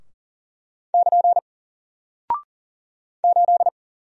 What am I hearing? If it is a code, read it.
Morse code, heard as a keyed tone, about 40 wpm, 700 Hz. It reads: <KN> 8
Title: Single Character (Letter:Number:Punctuation:Prosign) 40wpm